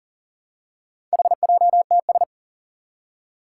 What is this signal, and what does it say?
Morse code, heard as a keyed tone, about 40 wpm, 700 Hz. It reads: HJTS
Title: Four Challenging Letter-Number Combinations 40wpm